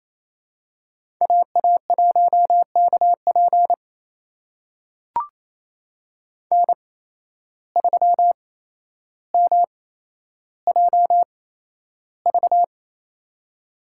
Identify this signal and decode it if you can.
Morse code, heard as a keyed tone, about 28 wpm, spaced as for 12 wpm, 700 Hz. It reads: AA1KP N3MJV
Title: USA Call Signs - 5 Characters - 8x Character Spacing - 28wpm